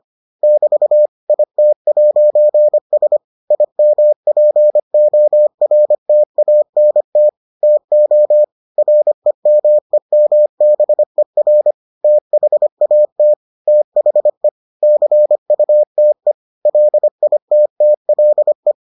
Morse code, heard as a keyed tone, about 25 wpm, 600 Hz. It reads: <BT> IT'S IMPORTANT TO REMEMBER THAT THE CUTE LITTLE